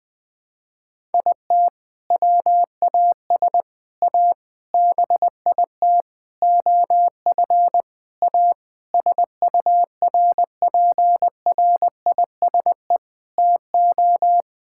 Morse code, heard as a keyed tone, about 20 wpm, 700 Hz. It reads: IT WAS A BIT OF A SURPRISE TO
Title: Sentences with 6-Word Phrases - 20wpm